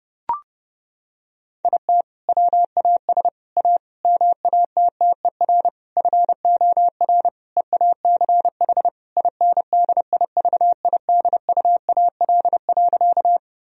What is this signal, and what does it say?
Morse code, heard as a keyed tone, about 30 wpm, 700 Hz. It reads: IT WAS A MATTER FOR EACH INDIVIDUAL.
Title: Sentences from Top 1000 Words - Review Part 4 - 30wpm